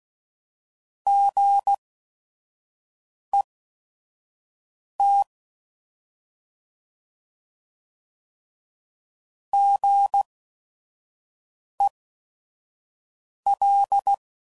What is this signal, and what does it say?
Morse code, heard as a keyed tone, about 16 wpm, spaced as for 5 wpm, 800 Hz. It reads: GET GEL